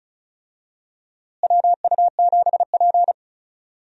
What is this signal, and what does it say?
Morse code, heard as a keyed tone, about 34 wpm, 700 Hz. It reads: WU7P
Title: USA Call Signs - 4 Characters - Repeated 2x - 35wpm